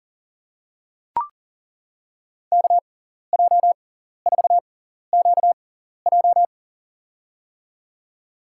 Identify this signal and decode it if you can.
Morse code, heard as a keyed tone, about 40 wpm, spaced as for 14 wpm, 700 Hz. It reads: KJ4QJ
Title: USA Call Signs - 6x Character Spacing - 40wpm